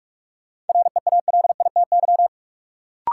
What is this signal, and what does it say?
Morse code, heard as a keyed tone, about 46 wpm, 700 Hz. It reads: REALITY
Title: Sentences from Top 1200 Words - New Word Drill - 45wpm